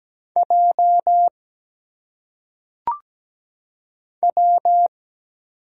Morse code, heard as a keyed tone, about 17 wpm, 700 Hz. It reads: J W
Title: Single Letters - Phonetic Pronunciation 17wpm